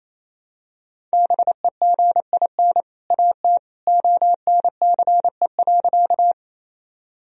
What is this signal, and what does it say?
Morse code, heard as a keyed tone, about 28 wpm, 700 Hz. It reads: BEGIN AT ONCE.
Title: Sentences from Top 1000 Words - Review Part 1 - 28wpm